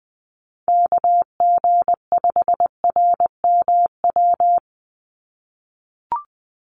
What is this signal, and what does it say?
Morse code, heard as a keyed tone, about 20 wpm, 700 Hz. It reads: KG5RMW
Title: USA Call Signs 20wpm